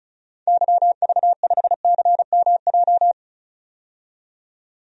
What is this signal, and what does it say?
Morse code, heard as a keyed tone, about 34 wpm, 700 Hz. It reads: YV5CMJ
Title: DXCC-100 Call Signs - 6 Characters - 35wpm